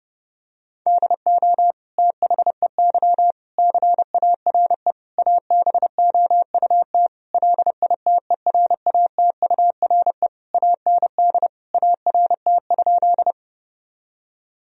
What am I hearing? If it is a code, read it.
Morse code, heard as a keyed tone, about 30 wpm, 700 Hz. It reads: DO THEY CARE ABOUT LITERATURE AND ART?
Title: Sentences from Top 1000 Words - Review Part 3 - 30wpm